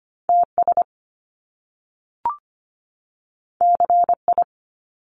Morse code, heard as a keyed tone, about 25 wpm, 700 Hz. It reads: TS CI
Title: Top Two Letter Combinations in Words 25wpm